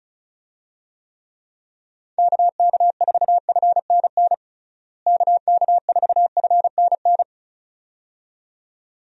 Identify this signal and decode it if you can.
Morse code, heard as a keyed tone, about 34 wpm, 700 Hz. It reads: KK4FNN KK4FNN
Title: USA Call Signs - 6 Characters - Repeated 2x - 35wpm